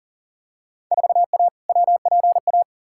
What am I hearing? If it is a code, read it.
Morse code, heard as a keyed tone, about 38 wpm, 700 Hz. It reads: 4A WPA
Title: Field Day Exchanges Encore - 40wpm